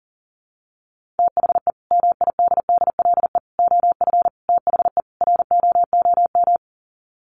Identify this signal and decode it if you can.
Morse code, heard as a keyed tone, about 40 wpm, 700 Hz. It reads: THE MIDDLE OF THE ROOM
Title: Sets of 5 Words Encore2 40wpm